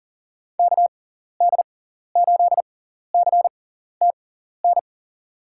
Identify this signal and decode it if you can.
Morse code, heard as a keyed tone, about 40 wpm, spaced as for 14 wpm, 700 Hz. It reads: KD8CTN
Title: USA Call Signs - 6 Characters - 6x Character Spacing - 40wpm